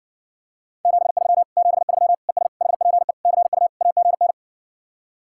Morse code, heard as a keyed tone, about 60 wpm, 700 Hz. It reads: 73 73 ES HPE CU AGN